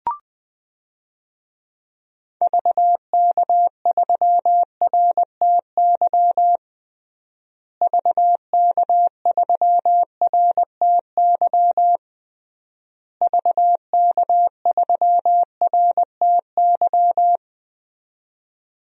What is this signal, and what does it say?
Morse code, heard as a keyed tone, about 20 wpm, 700 Hz. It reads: VK3RTY VK3RTY VK3RTY
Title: DXCC-100 Call Signs - 6 Characters - Repeated 3x - 20wpm